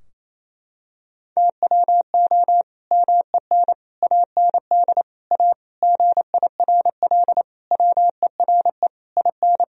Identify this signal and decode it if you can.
Morse code, heard as a keyed tone, about 28 wpm, 700 Hz. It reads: TWO MEN AND A GIRL WERE IN
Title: Sentences from Top 1000 Words - Review Part 3 - 28wpm